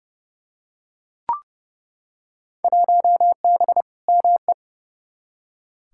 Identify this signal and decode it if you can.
Morse code, heard as a keyed tone, about 31 wpm, 700 Hz. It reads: 1B ME
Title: Field Day Exchanges - 30wpm